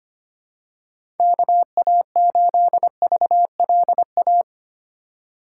Morse code, heard as a keyed tone, about 25 wpm, 700 Hz. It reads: KA8VLA
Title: USA Call Signs - 2x Character Spacing - 25wpm